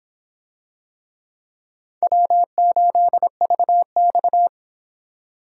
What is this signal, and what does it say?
Morse code, heard as a keyed tone, about 26 wpm, 700 Hz. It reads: W8VX